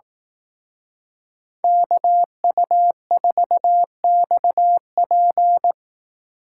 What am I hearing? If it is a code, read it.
Morse code, heard as a keyed tone, about 18 wpm, 700 Hz. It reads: KU4XP
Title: Speed Racing - USA Call Signs 15wpm